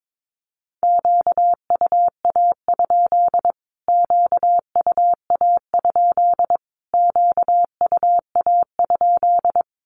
Morse code, heard as a keyed tone, about 22 wpm, 700 Hz. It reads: QUA? QUA? QUA?
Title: Q-Codes Repeated 3x 22wpm